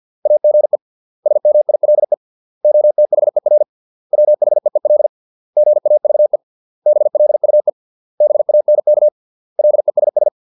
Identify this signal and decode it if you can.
Morse code, heard as a keyed tone, about 50 wpm, 600 Hz. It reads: AGE SMILE OTHER WHEEL GAVE BLUE BANK LESS
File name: 632 words / 632 Words 50WPM